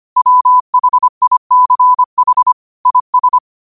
Morse code, heard as a keyed tone, about 25 wpm, 1000 Hz. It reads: WHICH IS
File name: qotdfiles / 25wpm